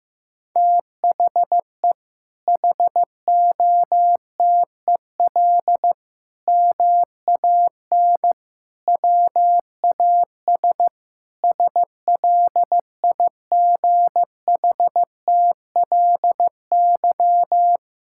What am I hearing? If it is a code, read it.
Morse code, heard as a keyed tone, about 15 wpm, 700 Hz. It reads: THE HOTEL MAN WAS SLIGHTLY
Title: Sentences from Top 1600 Words 15wpm